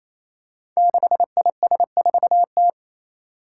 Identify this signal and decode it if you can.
Morse code, heard as a keyed tone, about 27 wpm, 700 Hz. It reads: 6IS4T
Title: Five Challenging Letter-Number Combinations 28wpm